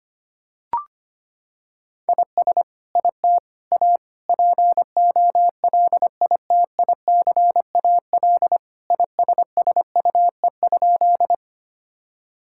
Morse code, heard as a keyed tone, about 25 wpm, 700 Hz. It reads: IS IT A POLITICAL ISSUE?
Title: Sentences from Top 1000 Words - Review Part 6 - 25wpm